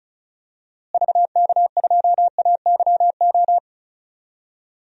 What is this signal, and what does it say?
Morse code, heard as a keyed tone, about 36 wpm, 700 Hz. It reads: VK2AYO
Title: DXCC-100 Call Signs - 6 Characters - Repeated 3x - 35wpm